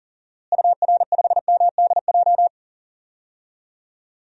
Morse code, heard as a keyed tone, about 38 wpm, 700 Hz. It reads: UR5MDJ
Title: DXCC-100 Call Signs - 6 Characters - Repeated 3x - 40wpm